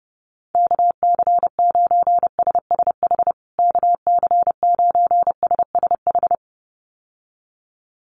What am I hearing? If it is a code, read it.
Morse code, heard as a keyed tone, about 30 wpm, 700 Hz. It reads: KC9SSH KC9SSH
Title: USA Call Signs Repeated 2x 30wpm